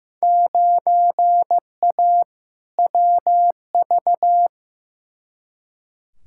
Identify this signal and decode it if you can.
Morse code, heard as a keyed tone, about 15 wpm, 700 Hz. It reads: 9A WV
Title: Field Day Exchanges - 15wpm